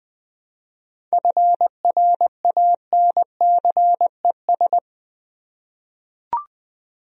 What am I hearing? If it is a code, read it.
Morse code, heard as a keyed tone, about 20 wpm, 700 Hz. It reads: FRANCES
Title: USA Common Names - 20wpm